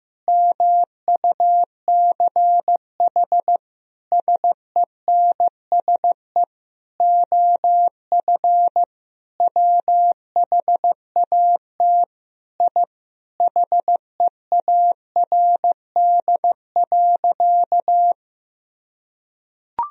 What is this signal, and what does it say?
Morse code, heard as a keyed tone, about 15 wpm, 700 Hz. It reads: MUCH SENSE OF WHAT I HEARD.
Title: Sentences from Top 1000 Words - Review Part 4 - 15wpm